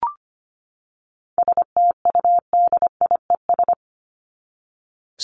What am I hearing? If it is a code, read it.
Morse code, heard as a keyed tone, about 25 wpm, 700 Hz. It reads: STUDIES
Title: ES:S Suffix - Top 100 Words with Base Words 25wpm